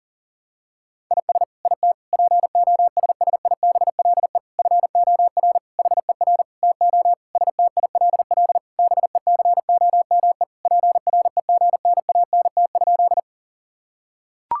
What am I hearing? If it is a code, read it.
Morse code, heard as a keyed tone, about 40 wpm, 700 Hz. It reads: IS IT POSSIBLE FOR HER TO STILL BECOME PREGNANT?
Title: Sentences from Top 1000 Words - Review Part 6 - 40wpm